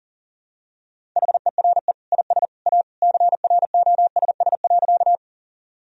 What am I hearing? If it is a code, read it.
Morse code, heard as a keyed tone, about 40 wpm, 700 Hz. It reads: HERE IS A CROSS.
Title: Sentences from Top 1000 Words - Review Part 1 - 40wpm